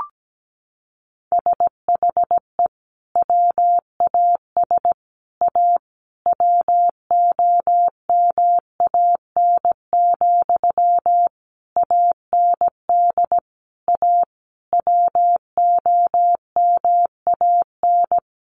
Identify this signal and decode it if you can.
Morse code, heard as a keyed tone, about 17 wpm, 700 Hz. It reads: SHE WAS A WOMAN, AND A WOMAN
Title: Sentences from Top 1400 Words 17wpm